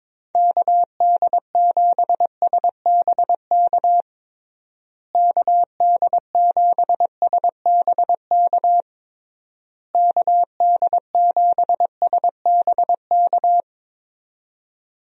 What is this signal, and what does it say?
Morse code, heard as a keyed tone, about 22 wpm, 700 Hz. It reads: KD7SBK KD7SBK KD7SBK
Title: USA Call Signs - 6 Characters - Repeated 3x - 22wpm